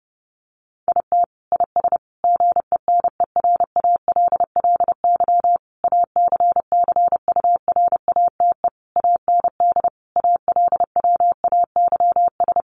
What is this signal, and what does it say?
Morse code, heard as a keyed tone, about 30 wpm, 700 Hz. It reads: IT IS GENERALLY ACCURATE AND ALWAYS
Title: Sentences from Top 1200 Words 30wpm